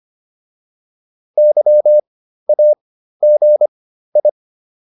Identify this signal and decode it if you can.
Morse code, heard as a keyed tone, about 25 wpm, spaced as for 13 wpm, 600 Hz. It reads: YAGI